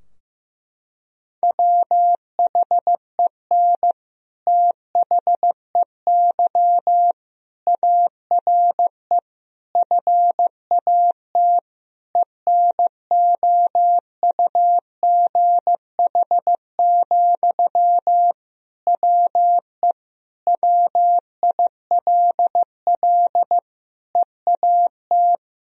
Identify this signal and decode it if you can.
Morse code, heard as a keyed tone, about 15 wpm, 700 Hz. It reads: WHEN THEY ARE FAT ENOUGH, WE WILL EAT
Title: Sentences from Top 1000 Words - Review Part 4 - 15wpm